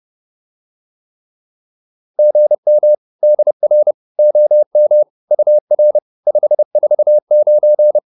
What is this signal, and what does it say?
Morse code, heard as a keyed tone, about 30 wpm, 600 Hz. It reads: GM DR OM UR 549